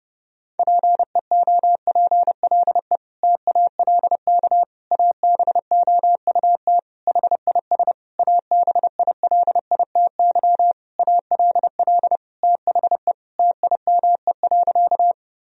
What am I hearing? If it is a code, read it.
Morse code, heard as a keyed tone, about 30 wpm, 700 Hz. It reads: PEOPLE TALK ABOUT HIS ABILITY ALL THE TIME.
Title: Sentences from Top 1000 Words - Review Part 3 - 30wpm